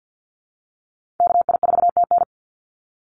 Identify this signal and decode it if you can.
Morse code, heard as a keyed tone, about 50 wpm, 700 Hz. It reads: KI4TN
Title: NA SOTA Activator Callsigns 50wpm